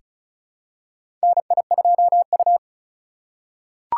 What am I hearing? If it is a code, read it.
Morse code, heard as a keyed tone, about 36 wpm, 700 Hz. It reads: NI2U
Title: USA Call Signs - 4 Characters - 3x Character Spacing - 35wpm